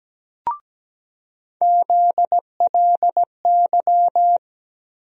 Morse code, heard as a keyed tone, about 17 wpm, 700 Hz. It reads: ZLY
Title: USA Call Sign Postfixes - 3 Letters 17wpm